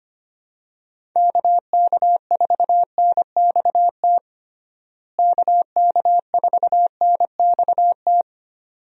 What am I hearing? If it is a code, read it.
Morse code, heard as a keyed tone, about 25 wpm, 700 Hz. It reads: KK4NXT KK4NXT
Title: USA Call Signs - 6 Characters - Repeated 2x - 25wpm